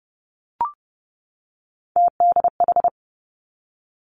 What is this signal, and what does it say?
Morse code, heard as a keyed tone, about 30 wpm, 700 Hz. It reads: TDH
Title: Three Challenging Letter-Number Combinations 30wpm